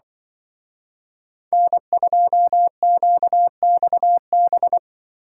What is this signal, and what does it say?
Morse code, heard as a keyed tone, about 24 wpm, 700 Hz. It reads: N2QXB
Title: Speed Racing - USA 1x3 Call Signs 20wpm